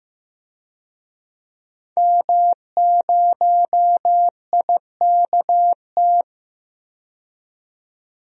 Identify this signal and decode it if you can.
Morse code, heard as a keyed tone, about 15 wpm, 700 Hz. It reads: M0IKT
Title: DXCC-100 Call Signs - 5 Characters - 15wpm